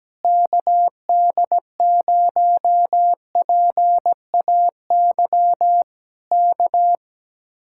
Morse code, heard as a keyed tone, about 17 wpm, 700 Hz. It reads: KD0PAY K
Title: Speed-Racing Common QSO Phrases Encore - 17wpm